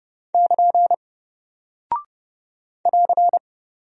Morse code, heard as a keyed tone, about 30 wpm, 700 Hz. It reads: <KN> <AR>
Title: Single Character (Letter:Number:Punctuation:Prosign) Alt 30wpm